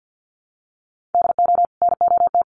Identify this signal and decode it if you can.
Morse code, heard as a keyed tone, about 48 wpm, 700 Hz. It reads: DO NOT
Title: Sets of 5 Words 50wpm